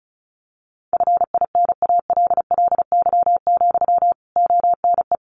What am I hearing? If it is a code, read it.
Morse code, heard as a keyed tone, about 34 wpm, 700 Hz. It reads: FINALLY, ONE
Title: Sentences from Top 1300 Words 35wpm